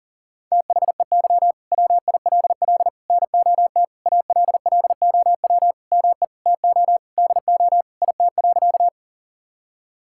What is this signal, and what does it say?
Morse code, heard as a keyed tone, about 40 wpm, 700 Hz. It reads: THEY WILL NOT ALLOW ME TO DO IT.
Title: Sentences from Top 1000 Words - Review Part 5 - 40wpm